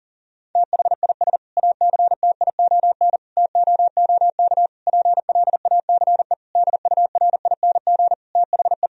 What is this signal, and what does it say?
Morse code, heard as a keyed tone, about 40 wpm, 700 Hz. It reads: THIS ACTION TOOK PLACE DURING THE
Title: Sentences from Top 1000 Words - Review Part 4 - 40wpm